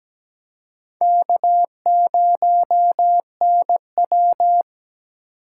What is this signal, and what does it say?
Morse code, heard as a keyed tone, about 17 wpm, 700 Hz. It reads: K0NW
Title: K1USN SST QSO Practice - 17wpm